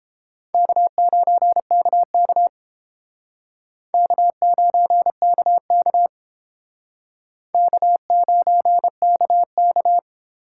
Morse code, heard as a keyed tone, about 29 wpm, 700 Hz. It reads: K9KK K9KK K9KK